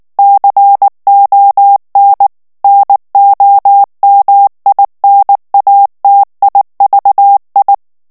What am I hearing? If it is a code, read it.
Morse code, heard as a keyed tone, about 19 wpm, 800 Hz. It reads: CON NOMINATIVI
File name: morse mp3 / ITALIEN texte clair 2 vitesse 18 mots minute